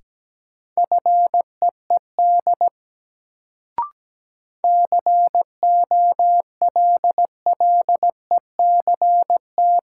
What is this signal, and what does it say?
Morse code, heard as a keyed tone, about 17 wpm, 700 Hz. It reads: FEED COLLECT
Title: Sentences from Top 1600 Words - New Word Drill - 17wpm